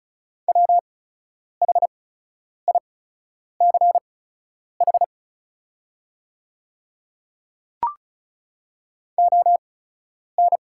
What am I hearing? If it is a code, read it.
Morse code, heard as a keyed tone, about 34 wpm, spaced as for 10 wpm, 690 Hz. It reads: WHICH ON
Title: 8x Character Spacing - Top 100 Words - 35wpm